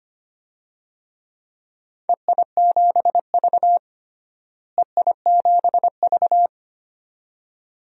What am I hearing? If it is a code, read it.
Morse code, heard as a keyed tone, about 25 wpm, 700 Hz. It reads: EI7V EI7V
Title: DXCC-100 Call Signs - 4 Characters - Repeated 2x - 25wpm